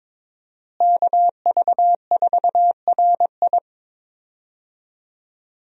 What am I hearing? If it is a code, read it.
Morse code, heard as a keyed tone, about 22 wpm, 700 Hz. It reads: KV4RI
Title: USA Call Signs - 5 Characters - Repeated 3x - 22wpm